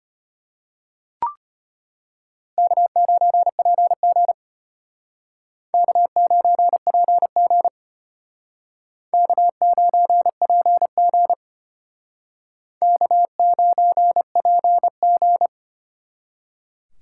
Morse code, heard as a keyed tone, about 33 wpm, 700 Hz. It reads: K9PG K9PG K9PG K9PG